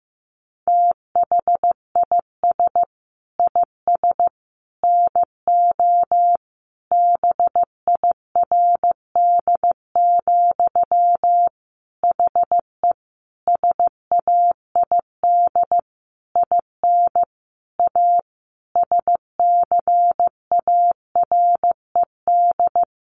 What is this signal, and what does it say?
Morse code, heard as a keyed tone, about 15 wpm, 700 Hz. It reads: THIS IS NO BIRD, HE SAID IN A SCARED
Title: Sentences from Top 700 Words 15wpm